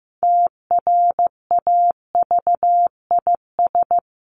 Morse code, heard as a keyed tone, about 15 wpm, 700 Hz. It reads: TRAVIS
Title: A Sound Of Thunder 15wpm